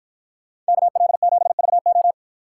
Morse code, heard as a keyed tone, about 52 wpm, 700 Hz. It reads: KL7VO